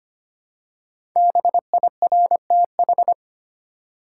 Morse code, heard as a keyed tone, about 25 wpm, 700 Hz. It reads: BIRTH
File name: 5 Letter Words - First 3 Letters Given - 25wpm